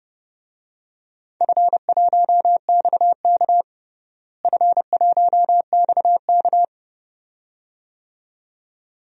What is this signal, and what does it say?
Morse code, heard as a keyed tone, about 30 wpm, 700 Hz. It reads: F1XK F1XK
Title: DXCC-100 Call Signs - 4 Characters - Repeated 2x - 30wpm